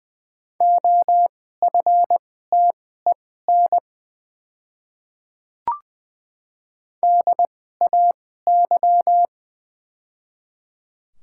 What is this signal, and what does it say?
Morse code, heard as a keyed tone, about 20 wpm, spaced as for 14 wpm, 700 Hz. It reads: OFTEN DAY
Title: 2x Character Spacing - Top 300 Words - 20wpm